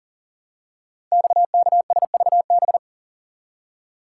Morse code, heard as a keyed tone, about 39 wpm, 700 Hz. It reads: XKSVB
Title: Code Groups - 5 Letters - 40wpm